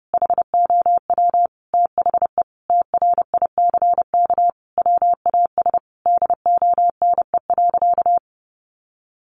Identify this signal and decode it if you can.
Morse code, heard as a keyed tone, about 30 wpm, 700 Hz. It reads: HOW THE TRICK WAS DONE.
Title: Sentences from Top 1500 Words 30wpm